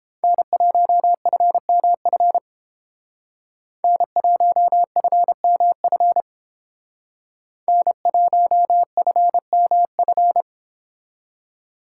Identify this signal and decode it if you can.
Morse code, heard as a keyed tone, about 30 wpm, 700 Hz. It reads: N1FMF N1FMF N1FMF